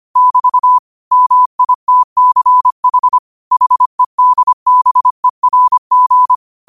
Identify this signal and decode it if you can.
Morse code, heard as a keyed tone, about 25 wpm, 1000 Hz. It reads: <BT> MITCH HEDBERG
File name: qotdfiles / 25wpm